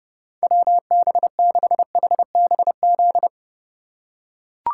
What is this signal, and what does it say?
Morse code, heard as a keyed tone, about 30 wpm, 700 Hz. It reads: WB6HBZ
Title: USA Call Signs - 6 Characters - 2x Character Spacing - 30wpm